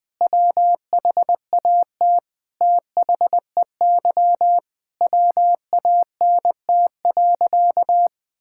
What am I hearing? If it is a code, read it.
Morse code, heard as a keyed tone, about 20 wpm, 700 Hz. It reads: WHAT THEY WANT.
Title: Sentences with 5-Word Phrases - 20wpm